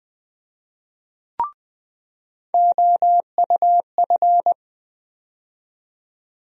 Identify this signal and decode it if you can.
Morse code, heard as a keyed tone, about 20 wpm, 700 Hz. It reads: OUF
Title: USA Call Sign Postfixes - 3 Letters 20wpm